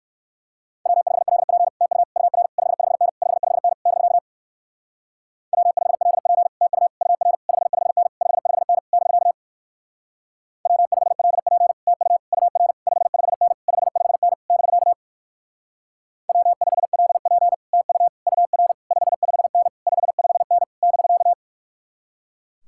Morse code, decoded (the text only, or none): W5LP TU UR 55N 55N <BK> W5LP TU UR 55N 55N <BK> W5LP TU UR 55N 55N <BK> W5LP TU UR 55N 55N <BK>